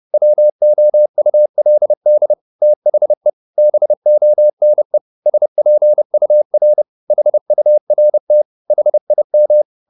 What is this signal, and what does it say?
Morse code, heard as a keyed tone, about 30 wpm, 600 Hz. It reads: WOULD THE BONE SPUR HURT HIM
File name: the old man and the sea / the old man and the sea 1 30wpm